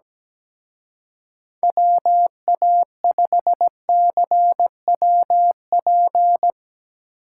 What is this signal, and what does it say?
Morse code, heard as a keyed tone, about 17 wpm, 700 Hz. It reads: WA5CWP
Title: Speed Racing - USA Call Signs 17wpm